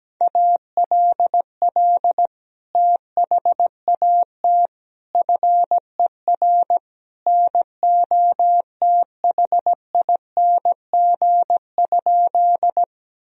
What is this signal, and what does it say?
Morse code, heard as a keyed tone, about 17 wpm, 700 Hz. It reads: ALL THAT FER NOTHING?
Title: Sentences from Top 500 Words and Abbreviations 17wpm